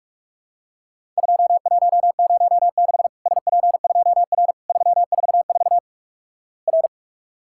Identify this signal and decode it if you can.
Morse code, heard as a keyed tone, about 48 wpm, 700 Hz. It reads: 2106 SP2R 344 R
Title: Worked All Europe DX Contest - QTC Exchange - 45wpm